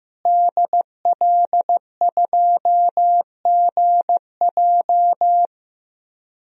Morse code, heard as a keyed tone, about 15 wpm, 700 Hz. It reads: DL2GJ
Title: DXCC-100 Call Signs - 5 Characters - Repeated 2x - 15wpm